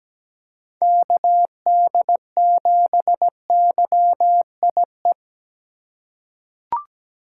Morse code, heard as a keyed tone, about 17 wpm, 700 Hz. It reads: KD7YIE
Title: USA Call Signs - 6 Characters - 4x Character Spacing - 17wpm